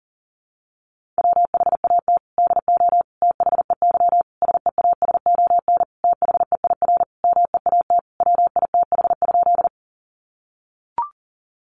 Morse code, heard as a keyed tone, about 40 wpm, 700 Hz. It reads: WHAT DO THEY SEASON THEIR MEAT WITH?
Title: Sentences from Top 1400 Words 40wpm